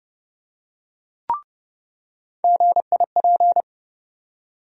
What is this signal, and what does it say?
Morse code, heard as a keyed tone, about 30 wpm, 700 Hz. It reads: GIP
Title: USA Call Sign Postfixes - 3 Letters 30wpm